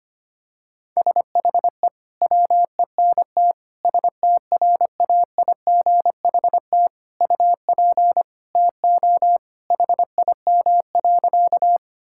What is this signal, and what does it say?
Morse code, heard as a keyed tone, about 25 wpm, 700 Hz. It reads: SHE WENT STRAIGHT UP TO HIM.
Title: Sentences from Top 1000 Words - Review Part 6 - 25wpm